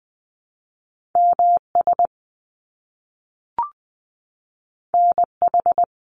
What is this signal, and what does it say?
Morse code, heard as a keyed tone, about 20 wpm, 700 Hz. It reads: MS NH
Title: US States Repeated 3x 20wpm